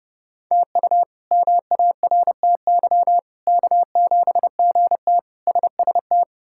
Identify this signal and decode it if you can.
Morse code, heard as a keyed tone, about 30 wpm, 700 Hz. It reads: TU MARTY K7GT SST
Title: K1USN SST QSO Practice - 30wpm